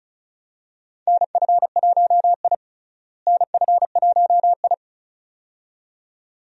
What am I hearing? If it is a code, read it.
Morse code, heard as a keyed tone, about 36 wpm, 700 Hz. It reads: NF1I NF1I
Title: USA Call Signs - 4 Characters - Repeated 2x - 35wpm